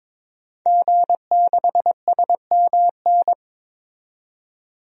Morse code, heard as a keyed tone, about 21 wpm, 700 Hz. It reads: G6SMN
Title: DXCC-100 Call Signs - 5 Characters - 22wpm